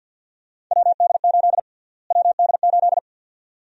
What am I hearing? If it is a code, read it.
Morse code, heard as a keyed tone, about 50 wpm, 700 Hz. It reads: WD8 WD8
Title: USA Call Sign Prefixes - Repeated 2x - 50wpm